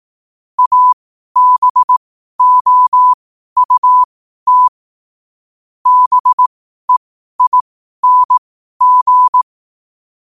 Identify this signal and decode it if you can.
Morse code, heard as a keyed tone, about 18 wpm, spaced as for 12 wpm, 1000 Hz. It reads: ABOUT BEING